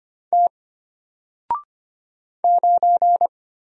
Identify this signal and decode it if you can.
Morse code, heard as a keyed tone, about 25 wpm, 700 Hz. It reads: T 9
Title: Single Letter-Number Alt 25wpm